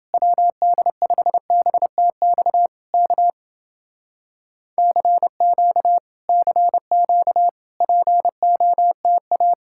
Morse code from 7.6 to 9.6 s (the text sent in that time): POTA